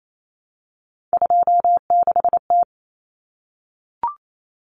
Morse code, heard as a keyed tone, about 28 wpm, 700 Hz. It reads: 26T
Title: Common Cut Numbers - 3-Digit 28wpm